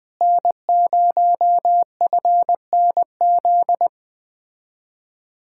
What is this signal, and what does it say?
Morse code, heard as a keyed tone, about 20 wpm, 700 Hz. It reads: N0FNZ
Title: Speed-Racing Common QSO Phrases - 17wpm